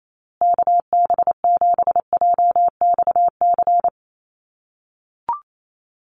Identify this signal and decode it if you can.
Morse code, heard as a keyed tone, about 28 wpm, 700 Hz. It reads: KB7JXC
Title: USA Call Signs Repeated 3x 28wpm